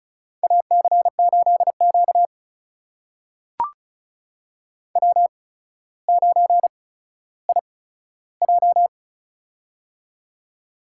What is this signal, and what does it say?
Morse code, heard as a keyed tone, about 36 wpm, spaced as for 15 wpm, 700 Hz. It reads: AC8Q W9IJ
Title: USA Call Signs - 4 Characters - 8x Character Spacing - 35wpm